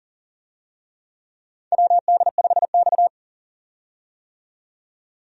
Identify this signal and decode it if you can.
Morse code, heard as a keyed tone, about 37 wpm, 700 Hz. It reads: WD5X
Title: USA Call Signs - 4 Characters - 40wpm